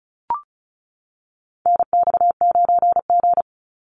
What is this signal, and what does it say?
Morse code, heard as a keyed tone, about 34 wpm, 700 Hz. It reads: NX9G
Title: Call Signs - 35wpm